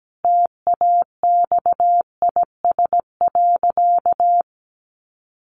Morse code, heard as a keyed tone, about 17 wpm, 700 Hz. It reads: TAXIS.
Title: Easy Phrases 17wpm